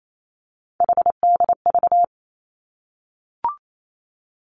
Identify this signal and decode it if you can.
Morse code, heard as a keyed tone, about 28 wpm, 700 Hz. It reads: HDV
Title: Three Challenging Letter-Number Combinations 28wpm